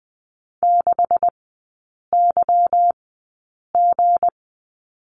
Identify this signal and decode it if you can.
Morse code, heard as a keyed tone, about 19 wpm, spaced as for 8 wpm, 700 Hz. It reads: 6YG
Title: 184 Lesson 30e - Letter b - Three Characters 20wpm